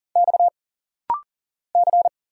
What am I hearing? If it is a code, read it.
Morse code, heard as a keyed tone, about 40 wpm, 700 Hz. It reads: X C
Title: Single Letters - Rapid-Fire 40wpm